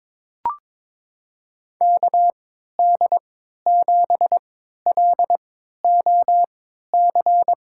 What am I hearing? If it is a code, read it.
Morse code, heard as a keyed tone, about 22 wpm, spaced as for 13 wpm, 700 Hz. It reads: KD7LOC
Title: USA Call Signs - 6 Characters - 3x Character Spacing - 22wpm